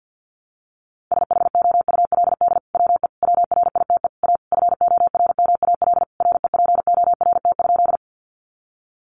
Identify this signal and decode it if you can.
Morse code, heard as a keyed tone, about 50 wpm, 700 Hz. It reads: SHOULD WE WRITE A FORMAL REPORT?
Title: Sentences from Top 1300 Words 50wpm